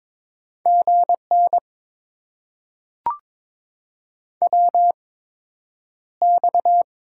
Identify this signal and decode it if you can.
Morse code, heard as a keyed tone, about 22 wpm, spaced as for 9 wpm, 700 Hz. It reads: GN WX
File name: Top 90 QSO Elements - 8x Character Spacing - 22wpm